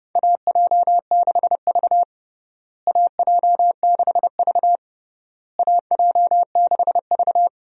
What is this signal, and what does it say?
Morse code, heard as a keyed tone, about 30 wpm, 700 Hz. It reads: AJ6V AJ6V AJ6V
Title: USA Call Signs - 4 Characters - Repeated 3x - 30wpm